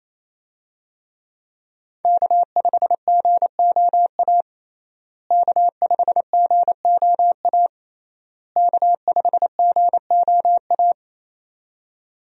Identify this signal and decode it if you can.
Morse code, heard as a keyed tone, about 27 wpm, 700 Hz. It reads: K5GOA K5GOA K5GOA
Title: USA Call Signs - 5 Characters - Repeated 3x - 28wpm